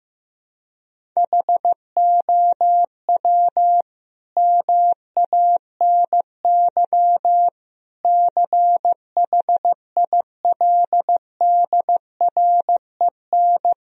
Morse code, heard as a keyed tone, about 15 wpm, 700 Hz. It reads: HOW MANY CHILDREN
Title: Sentences from Top 1000 Words - Review Part 1 - 15wpm